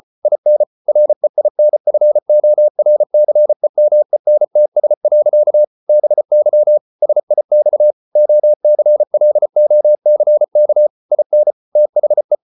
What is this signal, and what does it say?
Morse code, heard as a keyed tone, about 34 wpm, 600 Hz. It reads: IN REINFORCEMENTS. BY SIX OCLOCK IN THE